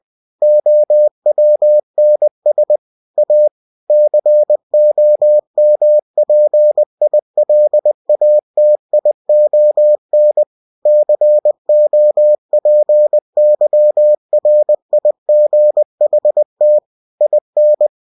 Morse code, heard as a keyed tone, about 20 wpm, 600 Hz. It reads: OWNS A COMPILATION COPYRIGHT IN
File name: cw-books / winnie-the-pooh / winnie-20wpm-page-0114